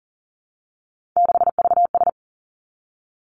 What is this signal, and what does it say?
Morse code, heard as a keyed tone, about 40 wpm, 700 Hz. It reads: 6VS
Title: Three Challenging Letter-Number Combinations 40wpm